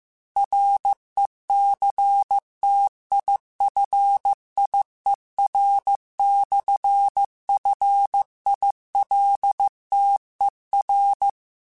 Morse code, heard as a keyed tone, about 15 wpm, 800 Hz. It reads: RECTIFIER/FILTER
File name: Technical Subjects / Technical Subjects-mp3s / Sw Pwr Sup @13